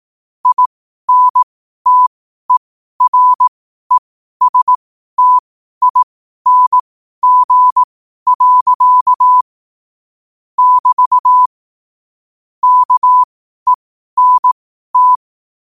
Morse code, heard as a keyed tone, about 18 wpm, spaced as for 12 wpm, 1000 Hz. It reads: INTERESTING. <BT> KENT
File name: qotdfiles / 12wpm